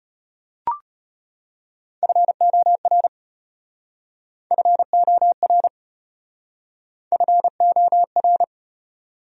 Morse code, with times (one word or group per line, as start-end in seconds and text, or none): 2.0-3.1 FOR
4.5-5.7 FOR
7.1-8.4 FOR